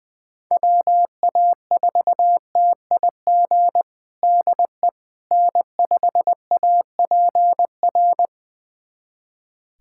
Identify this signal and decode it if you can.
Morse code, heard as a keyed tone, about 20 wpm, 700 Hz. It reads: WA4TIG DE N5APR
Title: USA Callsign DE Callsign 20wpm